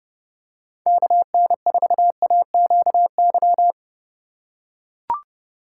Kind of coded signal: Morse code

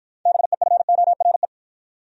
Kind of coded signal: Morse code

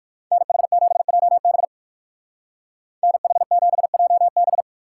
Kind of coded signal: Morse code

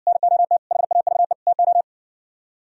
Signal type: Morse code

